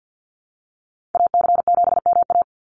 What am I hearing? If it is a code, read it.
Morse code, heard as a keyed tone, about 52 wpm, 700 Hz. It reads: AC7MA
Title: NA SOTA Activator Callsigns 50wpm